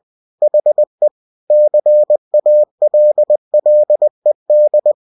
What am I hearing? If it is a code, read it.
Morse code, heard as a keyed tone, about 20 wpm, 600 Hz. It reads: HE CALLED
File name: cw-books / winnie-the-pooh / winnie-20wpm-page-0007